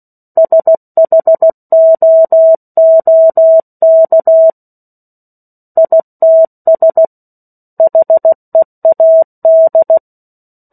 Morse code, heard as a keyed tone, about 16 wpm, 650 Hz. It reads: SHOOK ITS HEAD
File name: 03-Chapter-03 16 WPM